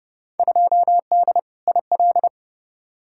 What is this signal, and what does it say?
Morse code, heard as a keyed tone, about 30 wpm, 700 Hz. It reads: 2D IL
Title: Field Day Exchanges - 30wpm